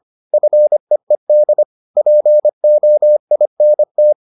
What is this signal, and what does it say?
Morse code, heard as a keyed tone, about 25 wpm, 600 Hz. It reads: FEED POINT